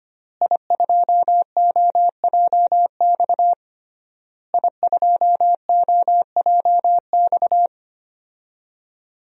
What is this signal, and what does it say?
Morse code, heard as a keyed tone, about 25 wpm, 700 Hz. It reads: I2OJX I2OJX
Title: DXCC-100 Call Signs - 5 Characters - Repeated 3x - 25wpm